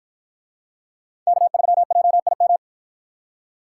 Morse code, heard as a keyed tone, about 53 wpm, 700 Hz. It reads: K3JIM